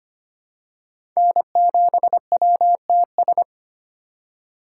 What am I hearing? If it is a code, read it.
Morse code, heard as a keyed tone, about 25 wpm, 700 Hz. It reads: N7WTS
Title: USA Call Signs - 2x Character Spacing - 25wpm